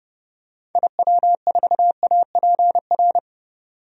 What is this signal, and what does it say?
Morse code, heard as a keyed tone, about 30 wpm, 700 Hz. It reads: IW4APR
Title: DXCC-100 Call Signs - 6 Characters - 30wpm